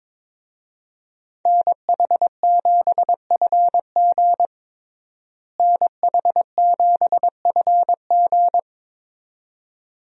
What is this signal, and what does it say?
Morse code, heard as a keyed tone, about 22 wpm, 700 Hz. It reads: NH7FG NH7FG
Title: USA Call Signs - 5 Characters - Repeated 2x - 22wpm